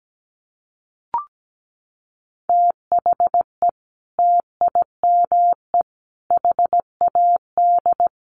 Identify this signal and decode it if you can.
Morse code, heard as a keyed tone, about 17 wpm, 700 Hz. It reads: THE TIME HAD
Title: Sentences from Top 100 Words 17wpm